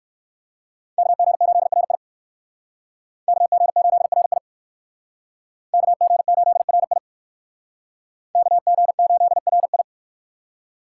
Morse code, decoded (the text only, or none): KG8RI KG8RI KG8RI KG8RI